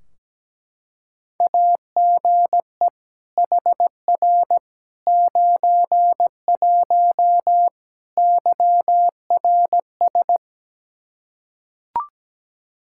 Morse code, heard as a keyed tone, about 17 wpm, 700 Hz. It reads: AGE HR 91 YRS
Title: Practice QSO - Exchange 3 - More Info - 17wpm